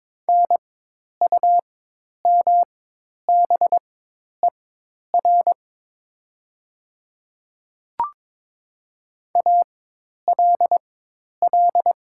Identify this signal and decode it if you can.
Morse code, heard as a keyed tone, about 22 wpm, spaced as for 10 wpm, 700 Hz. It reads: NUMBER ALL
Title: 4x Character Spacing - Top 100 Words - 22wpm